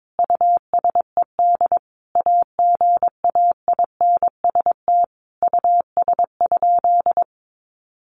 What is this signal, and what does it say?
Morse code, heard as a keyed tone, about 22 wpm, 700 Hz. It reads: USED AGAINST US?
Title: Sentences from Top 1500 Words 22wpm